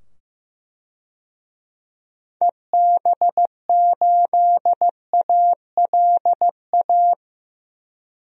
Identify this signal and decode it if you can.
Morse code, heard as a keyed tone, about 15 wpm, 700 Hz. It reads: EB8ALA
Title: DXCC-100 Call Signs - 6 Characters - 15wpm